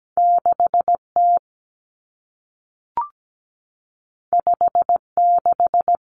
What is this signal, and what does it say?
Morse code, heard as a keyed tone, about 17 wpm, 700 Hz. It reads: 6T 56
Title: Common Cut Numbers - 2-Digit 17wpm